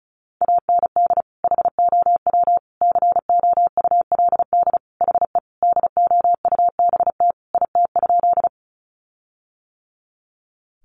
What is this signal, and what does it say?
Morse code, heard as a keyed tone, about 34 wpm, 700 Hz. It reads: AND HOW COULD HE DOUBT IT?
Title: Sentences from Top 1500 Words 35wpm